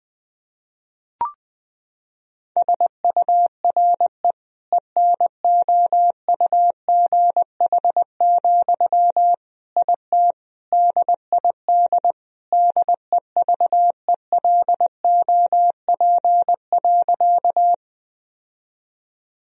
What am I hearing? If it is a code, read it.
Morse code, heard as a keyed tone, about 21 wpm, 700 Hz. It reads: SURE ENOUGH, IT DID DEVELOP.
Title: Sentences from Top 1000 Words - Review Part 2 - 20wpm